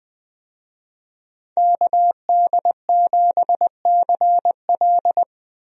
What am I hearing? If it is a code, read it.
Morse code, heard as a keyed tone, about 20 wpm, 700 Hz. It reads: KD7CL
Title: USA Call Signs - 5 Characters - 20wpm